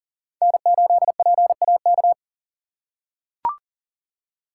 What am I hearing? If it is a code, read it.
Morse code, heard as a keyed tone, about 40 wpm, 700 Hz. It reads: N8PAK
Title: USA Call Signs - 4x Character Spacing - 40wpm